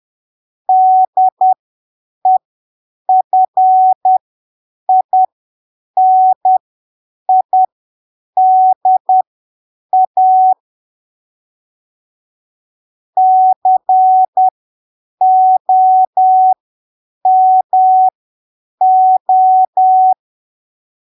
Morse code, heard as a keyed tone, about 10 wpm, spaced as for 7 wpm, 750 Hz. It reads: DEFINIDA COMO
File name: arquivos / texto / texto 18 10ppm